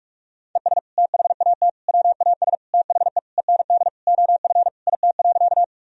Morse code, heard as a keyed tone, about 46 wpm, 700 Hz. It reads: ES THAT WAS THE END OF IT.
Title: Sentences from Top 200 Words and Abbreviations 45wpm